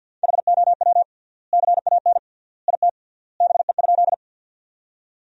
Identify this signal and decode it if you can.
Morse code, heard as a keyed tone, about 50 wpm, 700 Hz. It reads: HOW CAN IT BE?
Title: 3x Word-Spacing - Sentences from Top 100 Words - 50wpm